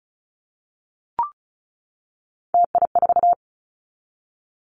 Morse code, heard as a keyed tone, about 36 wpm, 700 Hz. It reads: TI4
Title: Callsign Prefixes for Top 100 DXCC - 35wpm